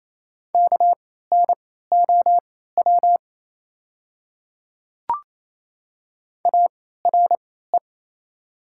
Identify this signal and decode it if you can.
Morse code, heard as a keyed tone, about 28 wpm, spaced as for 16 wpm, 700 Hz. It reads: KNOW ARE
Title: 3x Character Spacing - Top 200 Words - 28wpm